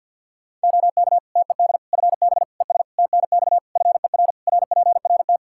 Morse code, heard as a keyed tone, about 49 wpm, 700 Hz. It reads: OK TED FB ES TNX FER RPRT
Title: Practice QSO - Exchange 2 - Info - 50wpm